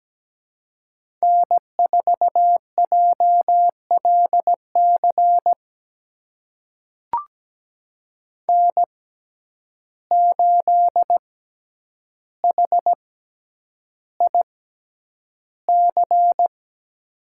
Morse code, heard as a keyed tone, about 17 wpm, spaced as for 9 wpm, 700 Hz. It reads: N4JLC N8HIC
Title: USA Call Signs - 6x Character Spacing - 17wpm